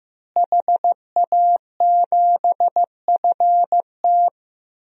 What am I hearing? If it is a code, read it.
Morse code, heard as a keyed tone, about 16 wpm, 700 Hz. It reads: HA7FT
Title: DXCC-100 Call Signs - 5 Characters - Repeated 3x - 15wpm